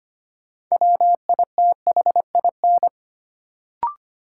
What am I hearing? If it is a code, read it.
Morse code, heard as a keyed tone, about 25 wpm, 700 Hz. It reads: WITHIN
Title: Sets of 2 Words - New Word Drill - 25wpm